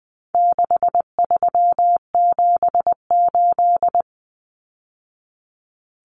Morse code, heard as a keyed tone, about 20 wpm, 700 Hz. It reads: 6378
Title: Numbers - 4-Digit 20wpm